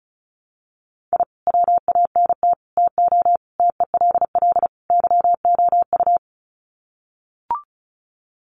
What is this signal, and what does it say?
Morse code, heard as a keyed tone, about 34 wpm, 700 Hz. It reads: I WANT TO TELL YOU
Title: Sets of 5 Words Encore 35wpm